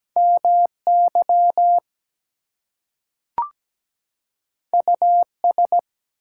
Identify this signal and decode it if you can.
Morse code, heard as a keyed tone, about 17 wpm, 700 Hz. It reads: MY US
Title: Two Letter Words - 17wpm